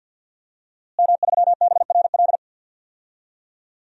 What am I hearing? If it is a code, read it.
Morse code, heard as a keyed tone, about 52 wpm, 700 Hz. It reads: M3BRL
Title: DXCC-100 Call Signs - 5 Characters - Repeated 3x - 50wpm